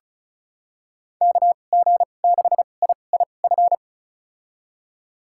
Morse code, heard as a keyed tone, about 34 wpm, spaced as for 25 wpm, 700 Hz. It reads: KG6IIF
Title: USA Call Signs - 2x Character Spacing - 35wpm